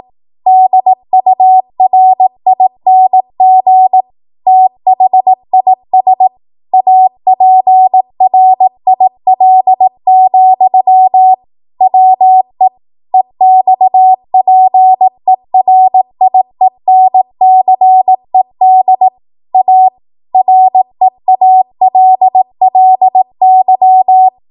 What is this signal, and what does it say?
Morse code, heard as a keyed tone, about 18 wpm, 750 Hz. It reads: DURING THIS APRIL, WE EXPERIENCED A REALLY
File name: Morse / 18WPM